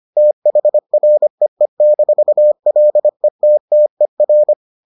Morse code, heard as a keyed tone, about 25 wpm, 600 Hz. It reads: THREE-LETTER